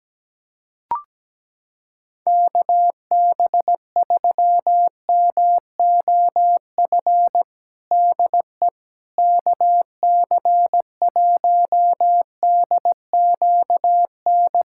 Morse code, heard as a keyed tone, about 17 wpm, 700 Hz. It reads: KB3MOF DE KC1DQN
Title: USA Callsign DE Callsign 17wpm